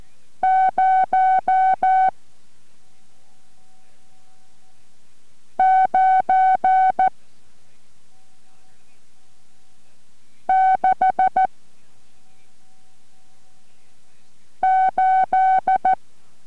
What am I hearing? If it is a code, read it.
Morse code, heard as a keyed tone, about 14 wpm, 750 Hz. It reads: 0 9 6 8